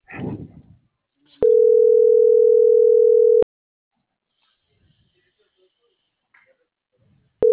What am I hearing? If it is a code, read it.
Ringback tone (North America), 440 + 480 Hz.
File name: CENTRAL SOTOVA / out roberto.miranda - .43422